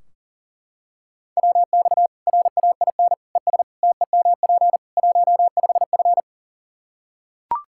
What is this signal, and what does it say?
Morse code, heard as a keyed tone, about 40 wpm, 700 Hz. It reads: WX RAIN ES TEMP 15F
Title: Practice QSO - Exchange 2 - Info - 40wpm